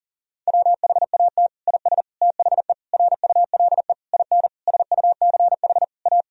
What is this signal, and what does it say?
Morse code, heard as a keyed tone, about 37 wpm, 700 Hz. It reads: WHAT IS THE RULE IN SUCH A
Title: Sentences from Top 1000 Words - Review Part 2 - 40wpm